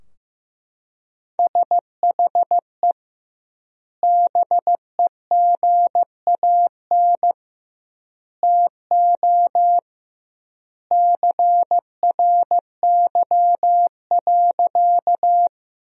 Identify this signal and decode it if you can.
Morse code, heard as a keyed tone, about 15 wpm, 700 Hz. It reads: SHE BEGAN TO CRY.
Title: 2x Word-Spacing - Sentences from Top 500 Words - 15wpm